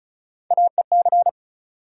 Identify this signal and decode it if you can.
Morse code, heard as a keyed tone, about 36 wpm, 700 Hz. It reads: AEC
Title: USA Call Sign Postfixes - 3 Letters 35wpm